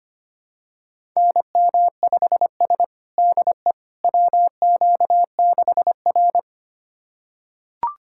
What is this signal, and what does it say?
Morse code, heard as a keyed tone, about 25 wpm, 700 Hz. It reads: NM5S DE WQ6R
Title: USA Callsign DE Callsign - 4 Characters - 25wpm